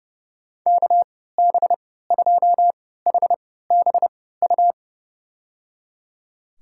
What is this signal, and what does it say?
Morse code, heard as a keyed tone, about 32 wpm, spaced as for 17 wpm, 700 Hz. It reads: KB2HBU
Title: USA Call Signs - 6 Characters - 3x Character Spacing - 30wpm